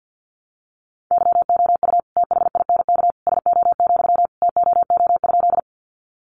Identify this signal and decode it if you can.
Morse code, heard as a keyed tone, about 48 wpm, 700 Hz. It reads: YOU THINK SO, TOO?
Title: Sentences from Top 200 Words 50wpm